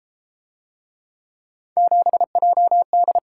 Morse code, heard as a keyed tone, about 31 wpm, 700 Hz. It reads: 7JD